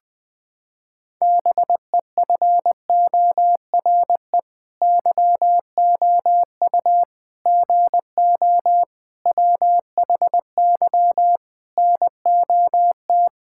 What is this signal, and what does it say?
Morse code, heard as a keyed tone, about 21 wpm, 700 Hz. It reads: BEFORE YOU GO WHY NOT
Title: Sets of 9 Words - 20wpm